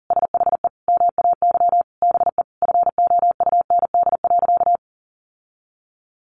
Morse code, heard as a keyed tone, about 40 wpm, 700 Hz. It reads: SHE MAY BE FOUND.
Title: Sentences from Top 1500 Words 40wpm